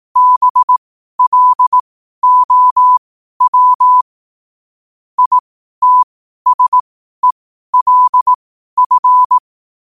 Morse code, heard as a keyed tone, about 18 wpm, spaced as for 12 wpm, 1000 Hz. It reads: BLOW ITSELF
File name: qotdfiles / 12wpm